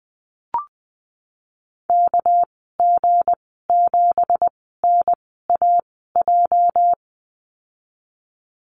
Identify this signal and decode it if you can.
Morse code, heard as a keyed tone, about 20 wpm, spaced as for 15 wpm, 700 Hz. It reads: KG7NAJ
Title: 213 Lesson 34g - Number 8 - Call Signs 20wpm